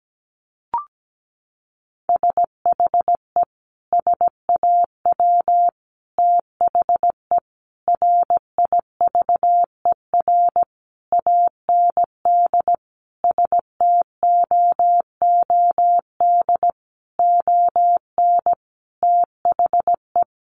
Morse code, heard as a keyed tone, about 17 wpm, 700 Hz. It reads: SHE SAW THE RIVER AND STOOD ON THE
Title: Sentences from Top 1200 Words 17wpm